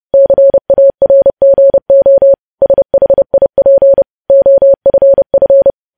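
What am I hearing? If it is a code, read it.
Morse code, heard as a keyed tone, about 30 wpm, 550 Hz. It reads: CARGO SHIP OFF
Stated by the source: Morse code news headlines practice sent at 30 wpm.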